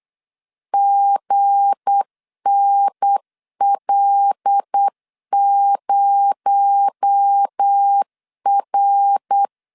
Morse code, heard as a keyed tone, about 8 wpm, 800 Hz. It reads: GNL0R
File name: CW / DET / 1 9 1500C S C 8wpm 5653